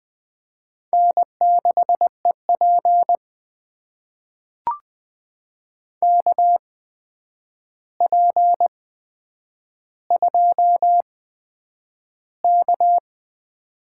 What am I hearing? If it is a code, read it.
Morse code, heard as a keyed tone, about 20 wpm, spaced as for 9 wpm, 700 Hz. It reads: N6EP KP2K
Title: USA Call Signs - 4 Characters - 8x Character Spacing - 20wpm